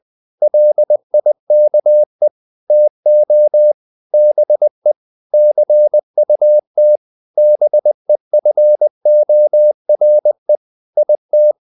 Morse code, heard as a keyed tone, about 20 wpm, 600 Hz. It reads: LIKE TO BE CUT BEFORE IT
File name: the old man and the sea / the old man and the sea part 1 20wpm